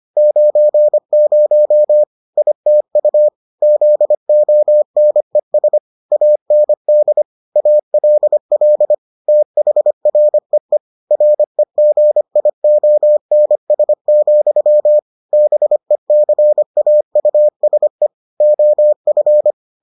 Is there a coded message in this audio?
Yes: Morse code from 0.2 to 19.5 s, heard as a keyed tone, about 25 wpm, 600 Hz. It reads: 90 ITU ZONES AND ALL THREE REGIONS, BECAUSE OF